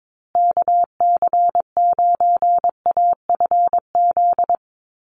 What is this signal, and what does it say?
Morse code, heard as a keyed tone, about 22 wpm, 700 Hz. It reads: KC9AFZ
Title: USA Call Signs Repeated 3x 22wpm